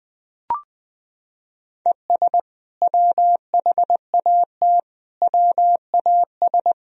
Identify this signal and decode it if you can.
Morse code, heard as a keyed tone, about 20 wpm, 700 Hz. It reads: ES WHAT WAS
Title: Sentences from Top 200 Words and Abbreviations 20wpm